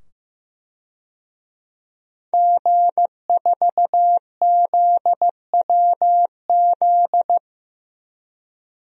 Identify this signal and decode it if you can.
Morse code, heard as a keyed tone, about 15 wpm, 700 Hz. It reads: G4ZWZ
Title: DXCC-100 Call Signs - 5 Characters - 15wpm